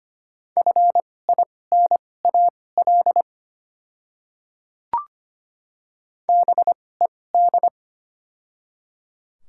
Morse code, heard as a keyed tone, about 25 wpm, spaced as for 18 wpm, 700 Hz. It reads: FINAL BED
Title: 2x Character Spacing - Top 500 Words - 25wpm